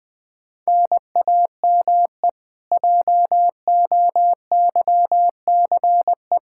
Morse code, heard as a keyed tone, about 20 wpm, 700 Hz. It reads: NAME JOYCE
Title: Practice QSO - Exchange 1 - Beginning - 20wpm